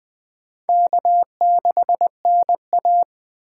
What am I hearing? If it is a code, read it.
Morse code, heard as a keyed tone, about 20 wpm, 700 Hz. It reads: K6NA
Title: K1USN SST QSO Practice - 20wpm